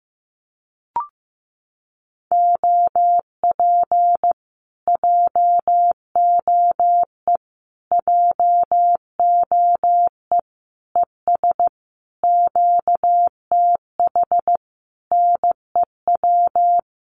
Morse code, heard as a keyed tone, about 14 wpm, 700 Hz. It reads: OP JOE JOE ES QTH NEW
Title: Common QSO Phrases Encore 15wpm